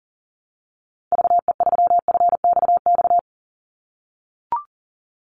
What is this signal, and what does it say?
Morse code, heard as a keyed tone, about 37 wpm, 700 Hz. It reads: VE3FXX
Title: Call Signs - 40wpm